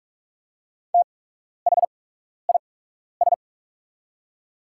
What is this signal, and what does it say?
Morse code, heard as a keyed tone, about 44 wpm, spaced as for 12 wpm, 690 Hz. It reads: THIS
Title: 8x Character Spacing - Top 100 Words - 45wpm